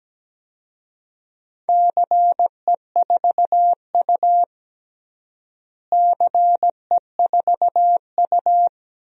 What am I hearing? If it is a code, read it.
Morse code, heard as a keyed tone, about 17 wpm, 700 Hz. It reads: CE4U CE4U
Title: DXCC-100 Call Signs - 4 Characters - Repeated 2x - 17wpm